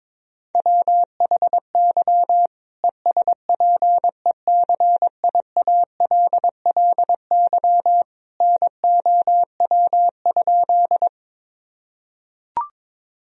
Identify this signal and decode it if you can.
Morse code, heard as a keyed tone, about 21 wpm, 700 Hz. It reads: WHY ESPECIALLY NOW?
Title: Sentences from Top 1000 Words - Review Part 4 - 22wpm